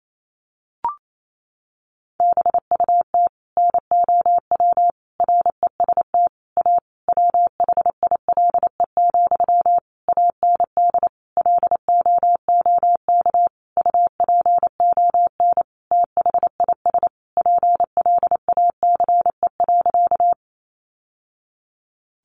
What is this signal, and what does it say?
Morse code, heard as a keyed tone, about 28 wpm, 700 Hz. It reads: BUT NOW REST A WHILE, AND LOOK UPON THIS PLACE.
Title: Sentences from Top 1100 Words 28wpm